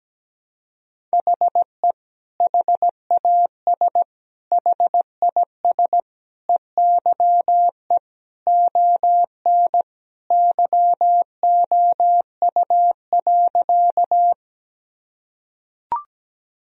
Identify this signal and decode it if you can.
Morse code, heard as a keyed tone, about 17 wpm, 700 Hz. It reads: HE HAS HIS EYE ON YOU.
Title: Sentences from Top 1000 Words - Review Part 1 - 17wpm